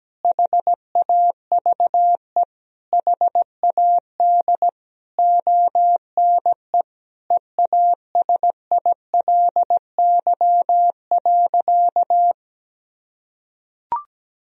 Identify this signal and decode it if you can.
Morse code, heard as a keyed tone, about 17 wpm, 700 Hz. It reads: HAVE HAD ONE EASILY.
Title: Sentences from Top 1000 Words - Review Part 3 - 17wpm